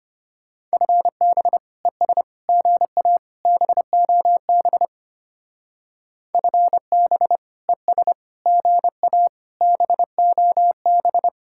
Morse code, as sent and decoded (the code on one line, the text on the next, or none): ..-. -... / . ... / --. .- / -... --- -... / ..-. -... / . ... / --. .- / -... --- -...
FB ES GA BOB FB ES GA BOB